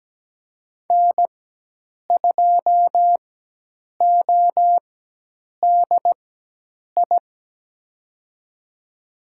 Morse code, heard as a keyed tone, about 17 wpm, spaced as for 8 wpm, 700 Hz. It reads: N2ODI
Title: USA Call Signs - 5 Characters - 4x Character Spacing - 17wpm